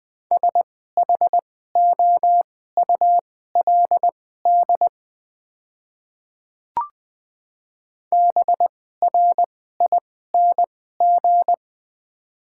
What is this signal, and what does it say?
Morse code, heard as a keyed tone, about 20 wpm, spaced as for 14 wpm, 700 Hz. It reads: SHOULD BRING
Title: 2x Character Spacing - Top 500 Words - 20wpm